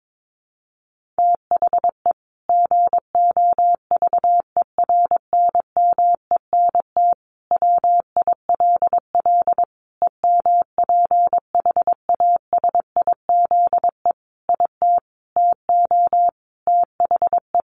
Morse code, heard as a keyed tone, about 22 wpm, 700 Hz. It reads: THE GOVERNMENT WILL EMPHASIZE IT TO THE
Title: Sentences from Top 1500 Words 22wpm